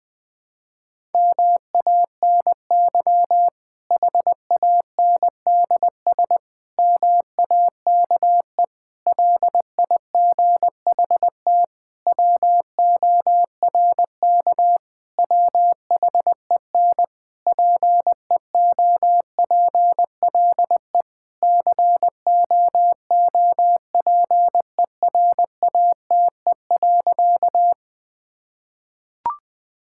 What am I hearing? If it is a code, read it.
Morse code, heard as a keyed tone, about 19 wpm, 700 Hz. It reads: MANY HANDS MAKE LIGHT WORK WHEN PEOPLE COOPERATE.
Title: Difficult Sentences with Hints - 20wpm